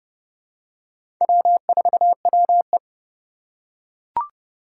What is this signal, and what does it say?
Morse code, heard as a keyed tone, about 32 wpm, 700 Hz. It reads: W4WE
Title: Four Challenging Letter-Number Combinations 30wpm